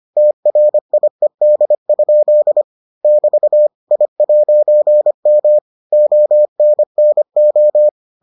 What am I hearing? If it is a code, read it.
Morse code, heard as a keyed tone, about 25 wpm, 600 Hz. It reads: TRIED? <BT> I'M ONNO